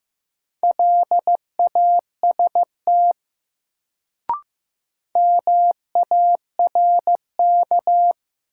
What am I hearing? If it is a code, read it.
Morse code, heard as a keyed tone, about 15 wpm, 700 Hz. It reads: LAST MARK
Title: Sentences from Top 300 Words - New Word Drill - 15wpm